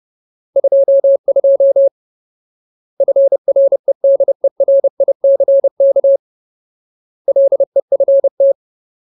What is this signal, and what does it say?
Morse code, heard as a keyed tone, about 30 wpm, 550 Hz. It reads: 22 FREDERICK LEFT
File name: Ultimate Word List / 30WPM Ultimate